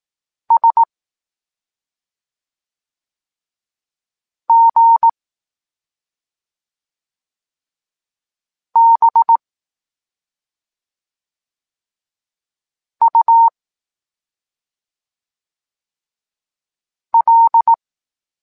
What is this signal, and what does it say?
Morse code, heard as a keyed tone, about 18 wpm, 920 Hz. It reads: S G B U L
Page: Tekst 23.3 – Ciąg dalszy doskonalenia odbioru poznanych liter